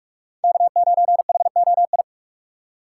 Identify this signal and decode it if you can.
Morse code, heard as a keyed tone, about 46 wpm, 700 Hz. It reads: K9HOI
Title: USA Call Signs - 5 Characters - 2x Character Spacing - 45wpm